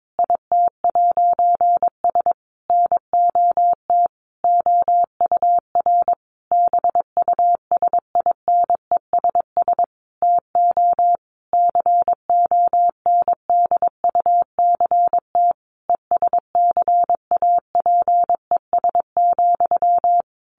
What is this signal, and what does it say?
Morse code, heard as a keyed tone, about 22 wpm, 700 Hz. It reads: IT'S NOT OUR BUSINESS TO CONDUCT ESCAPES,
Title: A Sound Of Thunder 22wpm